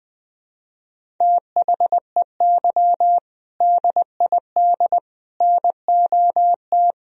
Morse code, heard as a keyed tone, about 20 wpm, 700 Hz. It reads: THEY DID NOT
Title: Sentences from Top 1000 Words - Review Part 1 - 20wpm